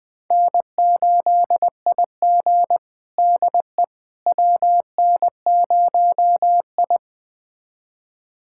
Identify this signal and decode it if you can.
Morse code, heard as a keyed tone, about 21 wpm, 700 Hz. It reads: N8IG DE WN0I
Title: USA Callsign DE Callsign - 4 Characters - 20wpm